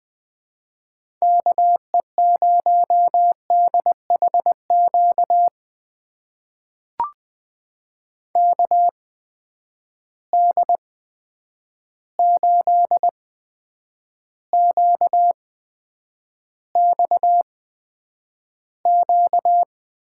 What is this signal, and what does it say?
Morse code, heard as a keyed tone, about 20 wpm, spaced as for 9 wpm, 700 Hz. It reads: KE0DHQ KD8QXQ
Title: USA Call Signs - 6 Characters - 8x Character Spacing - 20wpm